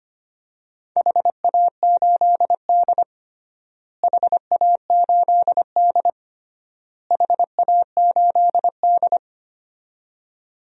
Morse code, heard as a keyed tone, about 25 wpm, 700 Hz. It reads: HA8D HA8D HA8D
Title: DXCC-100 Call Signs - 4 Characters - Repeated 3x - 25wpm